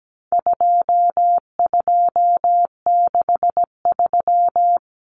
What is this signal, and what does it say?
Morse code, heard as a keyed tone, about 17 wpm, 700 Hz. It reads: 2263
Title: Common Cut Numbers - 4-Digit 17wpm